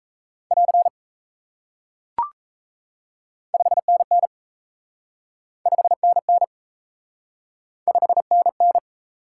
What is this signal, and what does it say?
Morse code, heard as a keyed tone, about 37 wpm, 700 Hz. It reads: <AR> 5NN 5NN 5NN